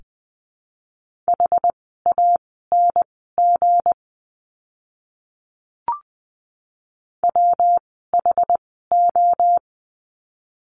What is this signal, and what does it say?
Morse code, heard as a keyed tone, about 19 wpm, spaced as for 14 wpm, 700 Hz. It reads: HANG WHO
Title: 160 Lesson 26f - Letter g - Words 20wpm